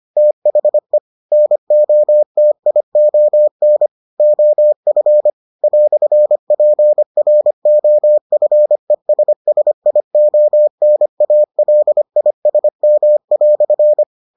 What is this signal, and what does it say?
Morse code, heard as a keyed tone, about 25 wpm, 600 Hz. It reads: THE NOTION OF "PROFESSIONALISM"